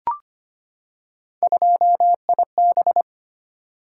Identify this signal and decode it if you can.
Morse code, heard as a keyed tone, about 25 wpm, 700 Hz. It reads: 2IB
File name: Three Letter-Number Combinations - 25wpm